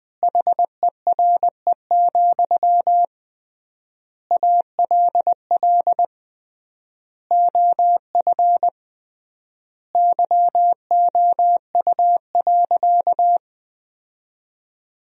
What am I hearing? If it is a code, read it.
Morse code, heard as a keyed tone, about 20 wpm, 700 Hz. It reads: HERE, ALL OF YOU.
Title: 3x Word-Spacing - Sentences from Top 500 Words - 20wpm